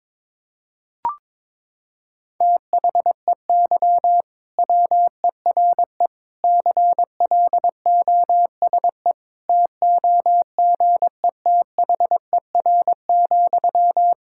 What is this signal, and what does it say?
Morse code, heard as a keyed tone, about 22 wpm, 700 Hz. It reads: THEY WERE CLOSE TOGETHER,
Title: Sentences from Top 1000 Words - Review Part 6 - 22wpm